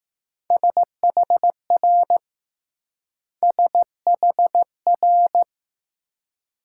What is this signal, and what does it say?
Morse code, heard as a keyed tone, about 20 wpm, 700 Hz. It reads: SHR SHR